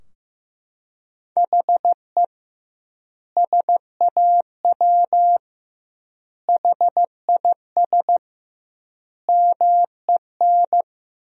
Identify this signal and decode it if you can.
Morse code, heard as a keyed tone, about 15 wpm, 700 Hz. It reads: HE SAW HIS MEN
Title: 2x Word-Spacing - Sentences from Top 300 Words - 15wpm